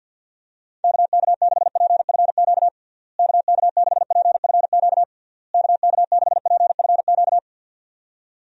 Morse code, heard as a keyed tone, about 50 wpm, 700 Hz. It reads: KK6PFQ KK6PFQ KK6PFQ
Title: USA Call Signs - 6 Characters - Repeated 3x - 50wpm